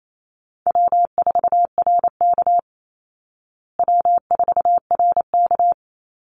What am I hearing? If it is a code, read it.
Morse code, heard as a keyed tone, about 28 wpm, 700 Hz. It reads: W4RK W4RK
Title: Easy Call Signs Repeated 3x - 28wpm